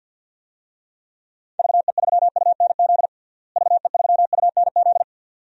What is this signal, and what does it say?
Morse code, heard as a keyed tone, about 47 wpm, 700 Hz. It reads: VE3UNZ VE3UNZ
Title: DXCC-100 Call Signs - 6 Characters - Repeated 2x - 50wpm